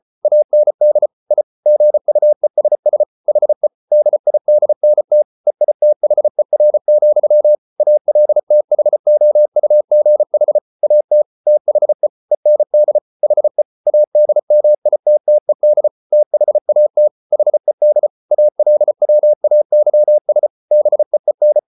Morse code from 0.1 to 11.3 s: AND I GUESS HE DIDNT EITHER, ALTHOUGH AT

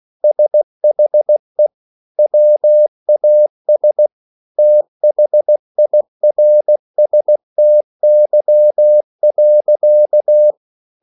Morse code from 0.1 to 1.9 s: SHE

